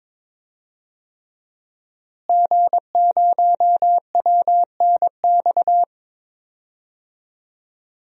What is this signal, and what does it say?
Morse code, heard as a keyed tone, about 22 wpm, 700 Hz. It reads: G0WNX
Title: DXCC-100 Call Signs - 5 Characters - Repeated 3x - 22wpm